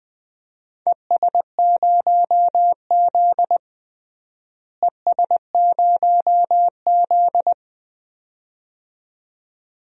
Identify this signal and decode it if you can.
Morse code, heard as a keyed tone, about 20 wpm, 700 Hz. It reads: ES0Z ES0Z
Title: DXCC-100 Call Signs - 4 Characters - Repeated 2x - 20wpm